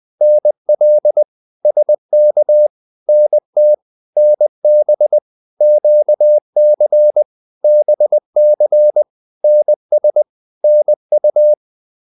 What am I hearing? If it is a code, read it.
Morse code, heard as a keyed tone, about 20 wpm, 600 Hz. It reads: NL SK NT NB QC BC NS NU
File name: Provinces-20